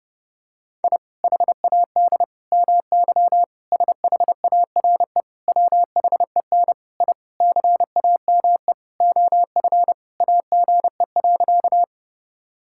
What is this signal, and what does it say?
Morse code, heard as a keyed tone, about 30 wpm, 700 Hz. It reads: I HAD MY SHARE WHEN I CAME OF AGE.
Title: Sentences from Top 1000 Words - Review Part 6 - 30wpm